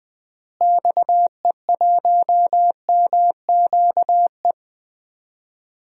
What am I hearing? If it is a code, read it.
Morse code, heard as a keyed tone, about 20 wpm, 700 Hz. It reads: XE1MQE
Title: DXCC-100 Call Signs - 6 Characters - Repeated 3x - 20wpm